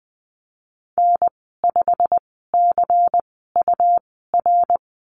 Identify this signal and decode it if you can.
Morse code, heard as a keyed tone, about 20 wpm, spaced as for 15 wpm, 700 Hz. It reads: N5CUR
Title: 161 Lesson 26g - Letter g - Call Signs 20wpm